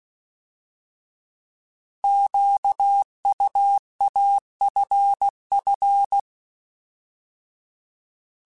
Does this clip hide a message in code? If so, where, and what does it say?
Yes: Morse code from 2.0 to 6.2 s, heard as a keyed tone, about 16 wpm, 800 Hz. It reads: QUAFF